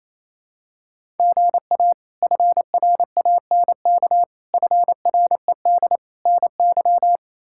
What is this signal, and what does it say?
Morse code, heard as a keyed tone, about 27 wpm, 700 Hz. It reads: GA FRANK FRED NY
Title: K1USN SST QSO Practice - 28wpm